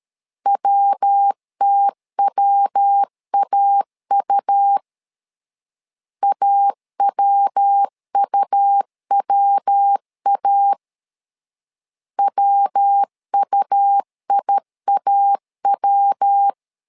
Morse code, heard as a keyed tone, about 13 wpm, 800 Hz. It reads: WTWAU AWUWA WUIAW
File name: LEZIONI / 3 MWRO / 3 MW 12wpm 1633